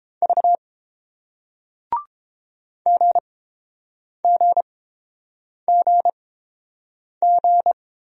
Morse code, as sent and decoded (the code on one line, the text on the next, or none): ...- / --. --. --. --.
V GGGG